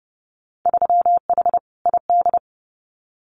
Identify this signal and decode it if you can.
Morse code, heard as a keyed tone, about 30 wpm, 700 Hz. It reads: 3H ID
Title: Winter Field Day Exchanges Encore 30wpm